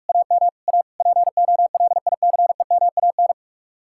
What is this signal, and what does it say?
Morse code, heard as a keyed tone, about 58 wpm, 700 Hz. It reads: AM A POLICEMAN